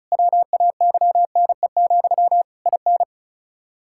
Morse code, heard as a keyed tone, about 36 wpm, 700 Hz. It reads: WAYNE, IN
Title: Practice QSO - Exchange 1 - Beginning - 35wpm